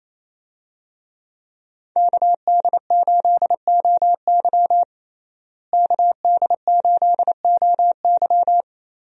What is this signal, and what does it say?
Morse code, heard as a keyed tone, about 28 wpm, 700 Hz. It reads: KD8OY KD8OY
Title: USA Call Signs - 5 Characters - Repeated 3x - 28wpm